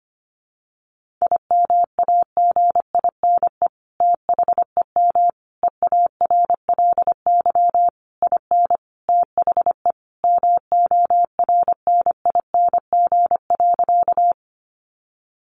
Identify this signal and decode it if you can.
Morse code, heard as a keyed tone, about 25 wpm, 700 Hz. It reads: IMAGINE THEM EARLY IN THE MORNING.
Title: Sentences from Top 1400 Words 25wpm